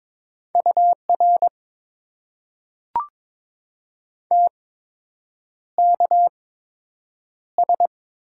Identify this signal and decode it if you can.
Morse code, heard as a keyed tone, about 22 wpm, spaced as for 8 wpm, 700 Hz. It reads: UR TKS
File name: Top 90 QSO Elements - 8x Character Spacing - 22wpm